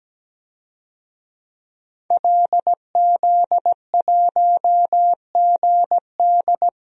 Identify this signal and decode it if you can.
Morse code, heard as a keyed tone, about 17 wpm, 700 Hz. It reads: LZ1GD
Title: DXCC-100 Call Signs - 5 Characters - 17wpm